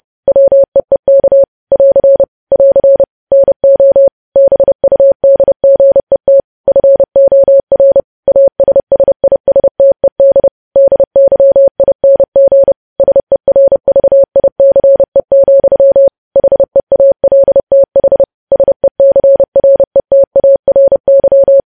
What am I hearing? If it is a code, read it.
Morse code, heard as a keyed tone, about 30 wpm, 550 Hz. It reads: WEEK <AR> <AR> NO BUDGET FOR ASSISTED DYING SERVICE, HEALTH SECRETARY
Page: news headlines in morse code at 30 wpm